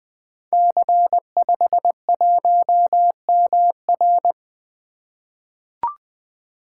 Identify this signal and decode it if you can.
Morse code, heard as a keyed tone, about 20 wpm, 700 Hz. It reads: C51MR
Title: Code Groups - 5 Letter:Numbers - 20wpm